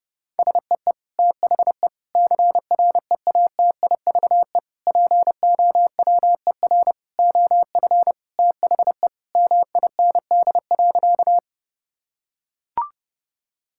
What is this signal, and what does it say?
Morse code, heard as a keyed tone, about 30 wpm, 700 Hz. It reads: SEE THE CREATIVE POWER OF THE MIND.
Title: Sentences from Top 1000 Words - Review Part 5 - 30wpm